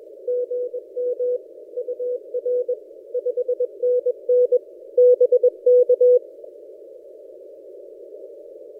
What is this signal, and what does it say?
Morse code, heard as a keyed tone, about 21 wpm, 490 Hz. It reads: GM UR 5NN BK
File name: posts / a35jr-12cw- -2336z